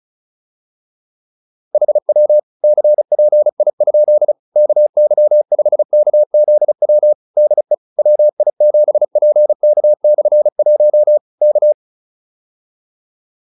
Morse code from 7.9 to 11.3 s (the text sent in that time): WI7PK/1